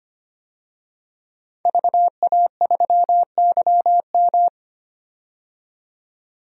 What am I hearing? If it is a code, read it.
Morse code, heard as a keyed tone, about 25 wpm, 700 Hz. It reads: VA3YM
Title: DXCC-100 Call Signs - 5 Characters - 25wpm